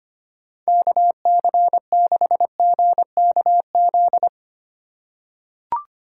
Morse code, heard as a keyed tone, about 25 wpm, 700 Hz. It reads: KC6GKZ
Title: USA Call Signs - 6 Characters - 3x Character Spacing - 25wpm